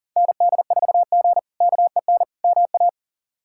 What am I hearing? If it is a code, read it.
Morse code, heard as a keyed tone, about 40 wpm, 700 Hz. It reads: ND4G KEN MA
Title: K1USN SST QSO Practice - 40wpm